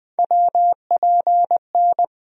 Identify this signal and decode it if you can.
Morse code, heard as a keyed tone, about 21 wpm, 700 Hz. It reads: WPN
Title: USA Call Sign Postfixes - 3 Letters 20wpm